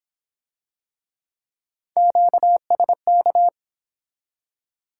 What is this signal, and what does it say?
Morse code, heard as a keyed tone, about 27 wpm, 700 Hz. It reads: QSK
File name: Speed-Racing - Three Letter-Number Combinations - 22wpm